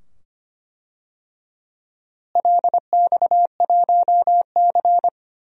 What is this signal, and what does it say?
Morse code, heard as a keyed tone, about 25 wpm, 700 Hz. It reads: LX1C
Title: DXCC-100 Call Signs - 4 Characters - Repeated 2x - 25wpm